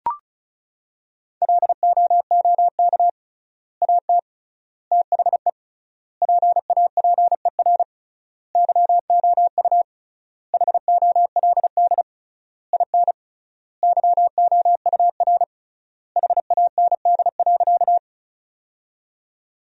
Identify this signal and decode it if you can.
Morse code, heard as a keyed tone, about 36 wpm, 700 Hz. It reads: LOOK AT THE PAPER YOU HOLD IN YOUR HAND.
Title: 3x Word-Spacing - Sentences from Top 400 Words - 35wpm